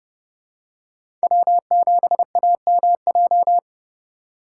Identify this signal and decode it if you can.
Morse code, heard as a keyed tone, about 30 wpm, 700 Hz. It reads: W7AMJ
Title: USA Call Signs - 5 Characters - 3x Character Spacing - 30wpm